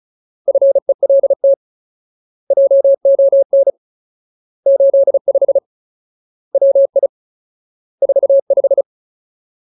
Morse code, heard as a keyed tone, about 34 wpm, 550 Hz. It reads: FELT JON 85 WI 45
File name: Ultimate Word List / 35WPM Ultimate